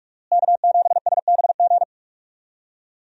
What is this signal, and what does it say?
Morse code, heard as a keyed tone, about 46 wpm, 700 Hz. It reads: K7SBG
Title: USA Call Signs - 5 Characters - Repeated 2x - 45wpm